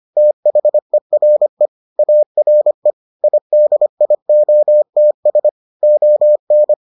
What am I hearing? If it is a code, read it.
Morse code, heard as a keyed tone, about 25 wpm, 600 Hz. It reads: THERE ARE IDIOTS ON